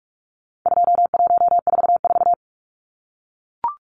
Morse code, heard as a keyed tone, about 44 wpm, 700 Hz. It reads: 2144
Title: Numbers - 4-Digit 45wpm